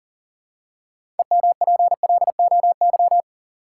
Morse code, heard as a keyed tone, about 40 wpm, 700 Hz. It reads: EMPLOY
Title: Six Letter Words - 40wpm